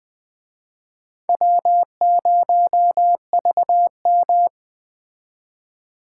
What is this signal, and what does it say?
Morse code, heard as a keyed tone, about 20 wpm, 700 Hz. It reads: W0VM
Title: Speed Racing - USA 1x2 Call Signs 17wpm